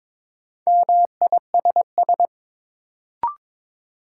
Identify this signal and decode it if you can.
Morse code, heard as a keyed tone, about 22 wpm, 700 Hz. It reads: MISS
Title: Sentences from Top 500 Words - New Word Drill - 22wpm